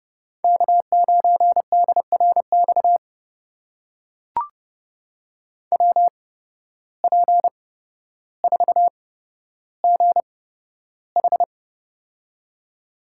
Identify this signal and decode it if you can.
Morse code, heard as a keyed tone, about 30 wpm, spaced as for 13 wpm, 700 Hz. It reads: K9DRX WP4GH
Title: USA Call Signs - 8x Character Spacing - 30wpm